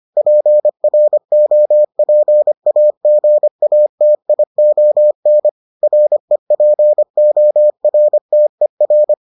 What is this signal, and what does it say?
Morse code, heard as a keyed tone, about 25 wpm, 600 Hz. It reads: PROPAGATION REPORTER